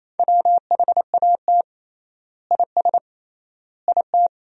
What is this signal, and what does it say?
Morse code, heard as a keyed tone, about 27 wpm, 700 Hz. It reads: WHAT IS IT
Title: 3x Word-Spacing - Sentences from Top 100 Words - 28wpm